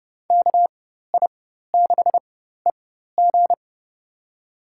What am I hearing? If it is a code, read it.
Morse code, heard as a keyed tone, about 30 wpm, spaced as for 14 wpm, 700 Hz. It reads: KI6EG
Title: USA Call Signs - 4x Character Spacing - 30wpm